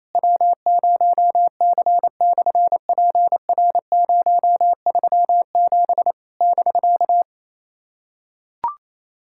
Morse code, heard as a keyed tone, about 28 wpm, 700 Hz. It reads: W0C/PR037 <BK>
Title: SOTA QSO Practice 28wpm